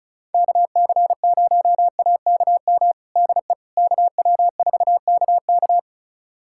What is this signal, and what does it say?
Morse code, heard as a keyed tone, about 36 wpm, 700 Hz. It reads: KC0AKM DE KW4KK
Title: USA Callsign DE Callsign 35wpm